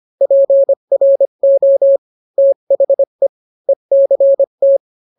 Morse code, heard as a keyed tone, about 25 wpm, 550 Hz. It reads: PRO THE ECT